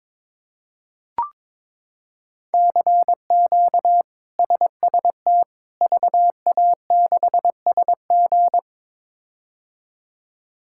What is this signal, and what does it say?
Morse code, heard as a keyed tone, about 22 wpm, 700 Hz. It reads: CQ SST VA6SG
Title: K1USN SST QSO Practice - 22wpm